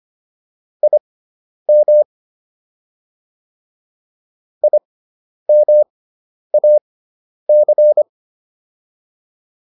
Morse code, heard as a keyed tone, about 25 wpm, spaced as for 10 wpm, 600 Hz. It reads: IM IMAC